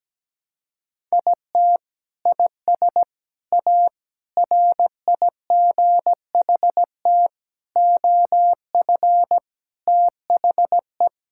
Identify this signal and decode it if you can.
Morse code, heard as a keyed tone, about 17 wpm, 700 Hz. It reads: IT IS A RIGHT OF THE
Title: Sentences from Top 1000 Words - Review Part 5 - 17wpm